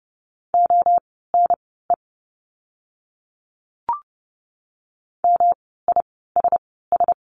Morse code, heard as a keyed tone, about 30 wpm, spaced as for 17 wpm, 700 Hz. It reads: ONE MISS
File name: course / 30:10wpm / 108 Lesson 18f - Letter m - Words 30wpm